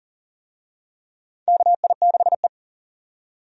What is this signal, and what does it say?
Morse code, heard as a keyed tone, about 40 wpm, 700 Hz. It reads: KI6E
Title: USA Call Signs - 4 Characters - 40wpm